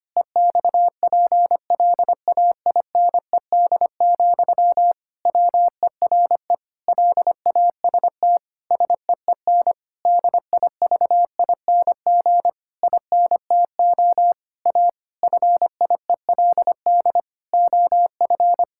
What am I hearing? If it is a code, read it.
Morse code, heard as a keyed tone, about 25 wpm, 700 Hz. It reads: EXPLAINED, WERE LAST SEEN DIVING INTO A FIELD OF
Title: Pangram Challenge v2 - 25wpm